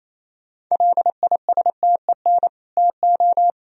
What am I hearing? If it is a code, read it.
Morse code, heard as a keyed tone, about 27 wpm, 700 Hz. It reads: LISTEN TO
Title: Sentences from Top 400 Words and Abbreviations 28wpm